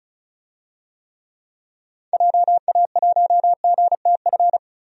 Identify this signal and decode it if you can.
Morse code, heard as a keyed tone, about 36 wpm, 700 Hz. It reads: JA1GTF
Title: DXCC-100 Call Signs - 6 Characters - 35wpm